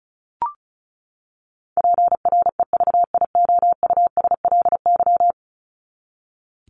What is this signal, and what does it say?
Morse code, heard as a keyed tone, about 34 wpm, 700 Hz. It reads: PREVIOUSLY
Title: LY Suffix - Top 100 Words with Base Words 35wpm